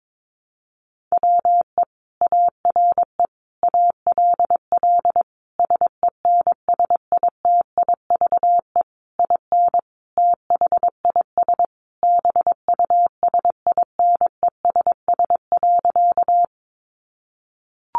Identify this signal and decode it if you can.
Morse code, heard as a keyed tone, about 22 wpm, 700 Hz. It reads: WE ARE ALL SENSITIVE IN THIS BUSINESS.
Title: Sentences from Top 1400 Words 22wpm